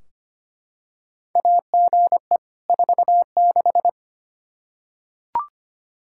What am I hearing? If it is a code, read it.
Morse code, heard as a keyed tone, about 25 wpm, 700 Hz. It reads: AGE 46
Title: Practice QSO - Exchange 3 - More Info - 25wpm